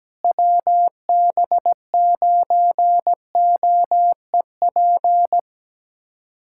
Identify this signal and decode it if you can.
Morse code, heard as a keyed tone, about 17 wpm, 700 Hz. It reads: WB9OEP
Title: USA Callsign DE Callsign - 6 Characters - 17wpm